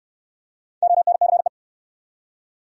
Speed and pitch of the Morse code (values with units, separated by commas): 69 wpm, 700 Hz